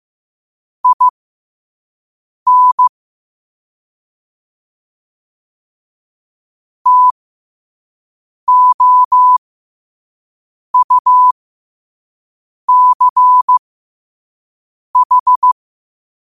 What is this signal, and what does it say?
Morse code, heard as a keyed tone, about 15 wpm, spaced as for 5 wpm, 1000 Hz. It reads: IN TOUCH
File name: qotdfiles / 5wpm